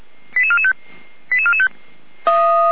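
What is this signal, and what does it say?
Five-tone selective call (ZVEI), heard as quick runs of five tones, tones of about 80 ms. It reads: 88445 88445
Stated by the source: ZVEI five-tone call received by SDR.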